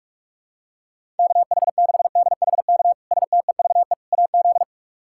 Morse code, heard as a keyed tone, about 46 wpm, 700 Hz. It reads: KH6DHK STEVE AZ
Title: K1USN SST QSO Practice - 45wpm